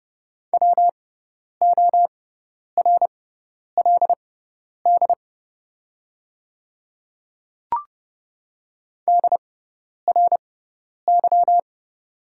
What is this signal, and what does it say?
Morse code, heard as a keyed tone, about 30 wpm, spaced as for 10 wpm, 700 Hz. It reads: WORLD DRY
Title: 6x Character Spacing - Top 500 Words - 30wpm